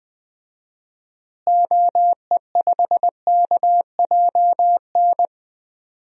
Morse code, heard as a keyed tone, about 20 wpm, 700 Hz. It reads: OE5KJN
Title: DXCC-100 Call Signs - 6 Characters - Repeated 3x - 20wpm